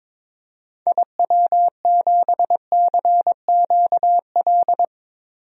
Morse code, heard as a keyed tone, about 22 wpm, 700 Hz. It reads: IW7CQL
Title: DXCC-100 Call Signs - 6 Characters - 22wpm